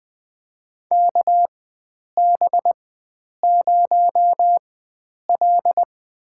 Morse code, heard as a keyed tone, about 20 wpm, spaced as for 9 wpm, 700 Hz. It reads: KB0L
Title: USA Call Signs - 4 Characters - 4x Character Spacing - 20wpm